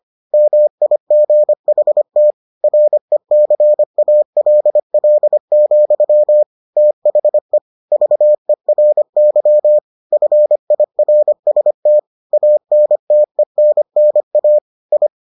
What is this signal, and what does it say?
Morse code, heard as a keyed tone, about 25 wpm, 600 Hz. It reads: MIGHT RECALL, THE VERY FIRST ANTENNA I